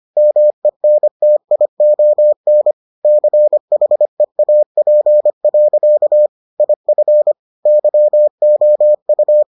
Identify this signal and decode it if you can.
Morse code, heard as a keyed tone, about 25 wpm, 600 Hz. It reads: MENTION CHEAP. IF YOU